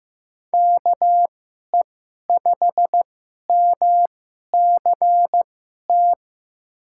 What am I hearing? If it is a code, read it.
Morse code, heard as a keyed tone, about 15 wpm, spaced as for 11 wpm, 700 Hz. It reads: KE5MCT
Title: USA Call Signs - 6 Characters - 2x Character Spacing - 15wpm